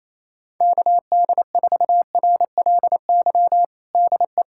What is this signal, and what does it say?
Morse code, heard as a keyed tone, about 28 wpm, 700 Hz. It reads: KD4RLY DE
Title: USA Callsign DE Callsign 28wpm